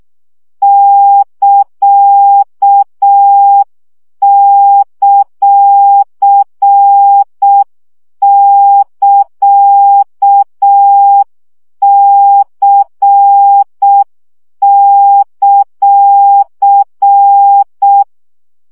Morse code, heard as a keyed tone, about 6 wpm, 800 Hz. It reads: <KA>;<KA>C;
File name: CW gyakorlas c ct pontosvesszo 5 baud